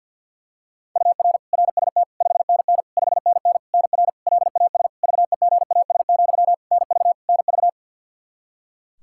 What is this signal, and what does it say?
Morse code, heard as a keyed tone, about 50 wpm, 700 Hz. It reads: UR RST 5NN 5NN NR LAS VEGAS, NV NV
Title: K1USN SST QSO Practice 50wpm